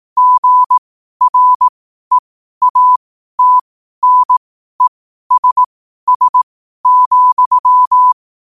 Morse code, heard as a keyed tone, about 18 wpm, spaced as for 12 wpm, 1000 Hz. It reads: GREATNESS,
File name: qotdfiles / 12wpm